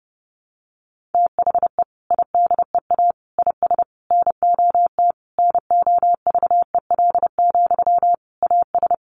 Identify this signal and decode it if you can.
Morse code, heard as a keyed tone, about 30 wpm, 700 Hz. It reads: THE IDEA IS NOT NOVEL, AS
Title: Sentences from Top 1300 Words 30wpm